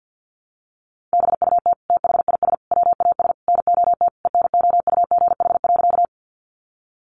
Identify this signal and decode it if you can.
Morse code, heard as a keyed tone, about 50 wpm, 700 Hz. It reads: BUT THIS WAS NOT ENOUGH.
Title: Sentences from Top 600 Words 50wpm